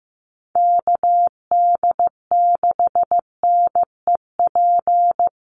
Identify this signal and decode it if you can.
Morse code, heard as a keyed tone, about 14 wpm, 700 Hz. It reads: KD6NEP
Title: USA Call Signs Repeated 3x 15wpm